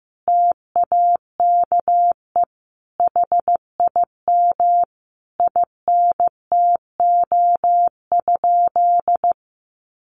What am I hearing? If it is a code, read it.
Morse code, heard as a keyed tone, about 15 wpm, 700 Hz. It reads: TAKE HIM INTO?
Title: Sentences from Top 600 Words 15wpm